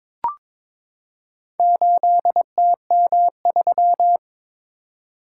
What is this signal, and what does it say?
Morse code, heard as a keyed tone, about 22 wpm, 700 Hz. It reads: 8TM3
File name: practice / 22wpm / Four Letter-Number Combinations - 22wpm